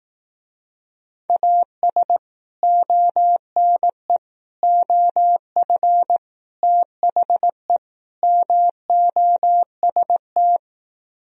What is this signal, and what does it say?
Morse code, heard as a keyed tone, about 18 wpm, 700 Hz. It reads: AS ONE OF THE MOST
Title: Speed-Racing - Sets of 5 Words 15wpm